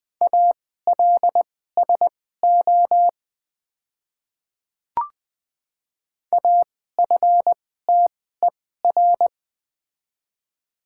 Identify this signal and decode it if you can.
Morse code, heard as a keyed tone, about 21 wpm, spaced as for 15 wpm, 700 Hz. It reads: ALSO AFTER
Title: 2x Character Spacing - Top 300 Words - 20wpm